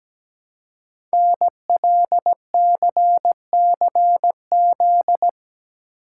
Morse code, heard as a keyed tone, about 17 wpm, 700 Hz. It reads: NLCCZ
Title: Code Groups - 5 Letters - 17wpm